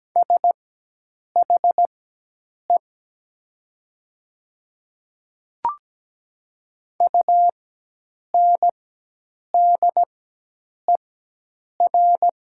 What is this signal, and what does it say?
Morse code, heard as a keyed tone, about 17 wpm, spaced as for 8 wpm, 700 Hz. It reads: SHE UNDER
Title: 4x Character Spacing - Top 200 Words - 17wpm